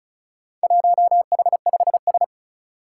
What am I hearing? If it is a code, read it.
Morse code, heard as a keyed tone, about 36 wpm, 700 Hz. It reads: 1H5S
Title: Four Challenging Letter-Number Combinations 35wpm